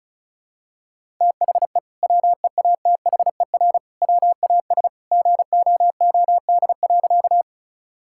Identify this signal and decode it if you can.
Morse code, heard as a keyed tone, about 34 wpm, 700 Hz. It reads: THE WEATHER WAS GOOD.
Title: Sentences from Top 1000 Words - Review Part 5 - 35wpm